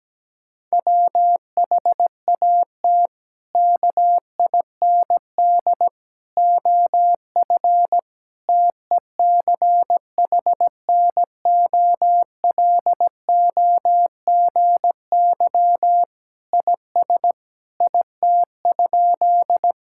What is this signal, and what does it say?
Morse code, heard as a keyed tone, about 17 wpm, 700 Hz. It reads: WHAT KIND OF TECHNOLOGY IS IT?
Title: Sentences from Top 1000 Words - Review Part 5 - 17wpm